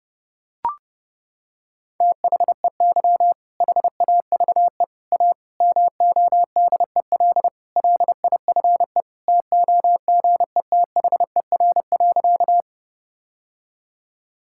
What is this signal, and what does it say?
Morse code, heard as a keyed tone, about 30 wpm, 700 Hz. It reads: THEY HAVE A MODEL LIFE TOGETHER.
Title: Sentences from Top 1000 Words - Review Part 5 - 30wpm